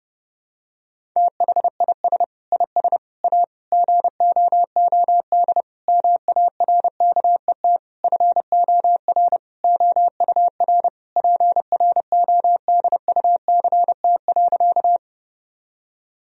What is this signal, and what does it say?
Morse code, heard as a keyed tone, about 30 wpm, 700 Hz. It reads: THIS IS A GOOD MARKET FOR OUR PRODUCT.
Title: Sentences from Top 1000 Words - Review Part 3 - 30wpm